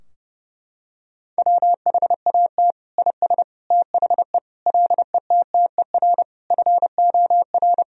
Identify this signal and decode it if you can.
Morse code, heard as a keyed tone, about 30 wpm, 700 Hz. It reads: WHAT IS THE LETTER FOR
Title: Sentences from Top 1000 Words - Review Part 2 - 30wpm